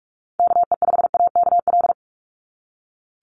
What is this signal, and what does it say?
Morse code, heard as a keyed tone, about 46 wpm, 700 Hz. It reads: KE5AKL
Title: NA SOTA Activator Callsigns 45wpm